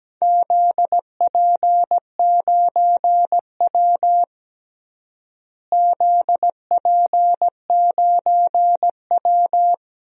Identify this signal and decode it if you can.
Morse code, heard as a keyed tone, about 17 wpm, 700 Hz. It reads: ZP9W ZP9W
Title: DXCC-100 Call Signs - 4 Characters - Repeated 3x - 17wpm